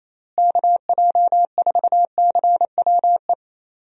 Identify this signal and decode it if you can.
Morse code, heard as a keyed tone, about 28 wpm, 700 Hz. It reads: KJ4CWE
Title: USA Call Signs - 6 Characters - Repeated 2x - 28wpm